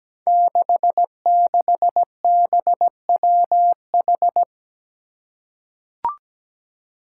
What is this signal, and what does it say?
Morse code, heard as a keyed tone, about 17 wpm, 700 Hz. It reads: 66BWH
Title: Five Challenging Letter-Number Combinations 17wpm